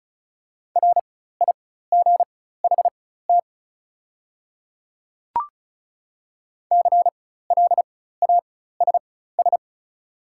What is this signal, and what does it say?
Morse code, heard as a keyed tone, about 36 wpm, spaced as for 16 wpm, 700 Hz. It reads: RIGHT CLASS
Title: 4x Character Spacing - Top 500 Words - 35wpm